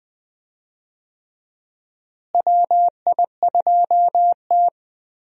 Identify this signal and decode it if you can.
Morse code, heard as a keyed tone, about 20 wpm, 700 Hz. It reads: WI2T
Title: USA Call Signs - 4 Characters - Repeated 3x - 20wpm